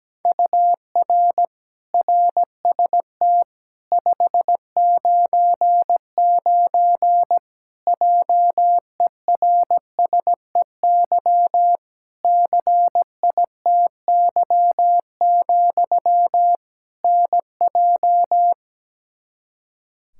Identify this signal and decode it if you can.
Morse code, heard as a keyed tone, about 17 wpm, 700 Hz. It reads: UR RST 599 JERSEY CITY, NJ
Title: Practice QSO - Exchange 1 - Beginning - 17wpm